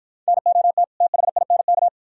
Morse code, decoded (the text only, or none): NOT THINK